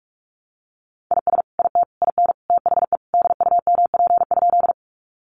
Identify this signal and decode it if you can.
Morse code, heard as a keyed tone, about 45 wpm, 700 Hz. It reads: IS IT IN THE DUMP?
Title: Sentences from Top 1300 Words 45wpm